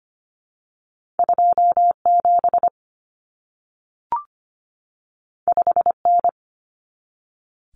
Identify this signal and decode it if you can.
Morse code, heard as a keyed tone, about 25 wpm, 700 Hz. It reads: 2 7 5 N
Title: Common Cut Numbers - 2-Digit 25wpm